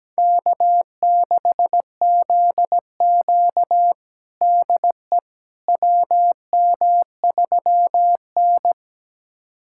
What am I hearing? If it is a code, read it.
Morse code, heard as a keyed tone, about 17 wpm, 700 Hz. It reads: K6ZQ DE WM3N
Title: USA Callsign DE Callsign - 4 Characters - 17wpm